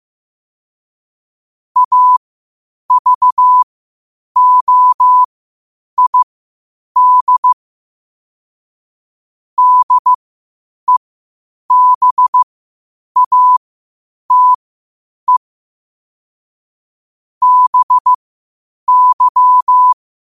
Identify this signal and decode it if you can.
Morse code, heard as a keyed tone, about 15 wpm, spaced as for 8 wpm, 1000 Hz. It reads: AVOID DEBATE BY
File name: qotdfiles / 8wpm